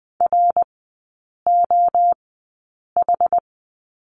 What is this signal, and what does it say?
Morse code, heard as a keyed tone, about 20 wpm, spaced as for 8 wpm, 700 Hz. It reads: ROH
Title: 060 Lesson 11e - Letter h - Three Characters 20wpm